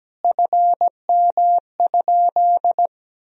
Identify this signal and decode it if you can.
Morse code, heard as a keyed tone, about 17 wpm, 700 Hz. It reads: FM?
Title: Sentences from Top 300 Words and Abbreviations 17wpm